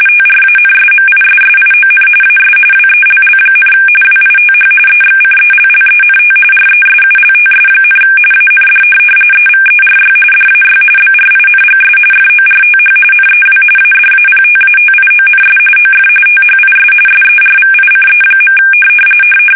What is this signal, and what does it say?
RTTY, about 45.45 baud. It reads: TRANSMISSION AS WELL, WITH EXCELLENT RESULTS, AND EVERY DAY BRINGS FORTH NEW ADAPTATIONS OF THIS SERVICE.
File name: fsk850